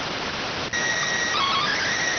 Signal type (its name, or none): SSTV (slow-scan television)